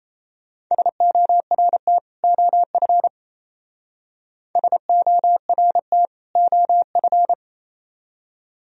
Morse code, heard as a keyed tone, about 32 wpm, 700 Hz. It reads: SORT OF SORT OF
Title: Speed-Racing - Sets of 2 Words 28wpm